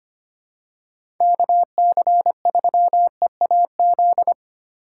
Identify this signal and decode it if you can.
Morse code, heard as a keyed tone, about 25 wpm, 700 Hz. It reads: KC3EAZ
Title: USA Call Signs - 6 Characters - 3x Character Spacing - 25wpm